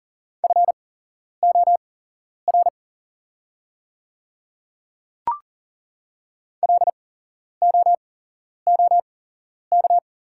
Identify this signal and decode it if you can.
Morse code, heard as a keyed tone, about 40 wpm, spaced as for 11 wpm, 700 Hz. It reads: FOR LOOK
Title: 8x Character Spacing - Top 100 Words - 40wpm